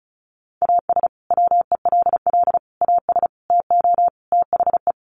Morse code, heard as a keyed tone, about 34 wpm, 700 Hz. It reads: AS WELL AS TO THE
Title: Sets of 5 Words Encore2 35wpm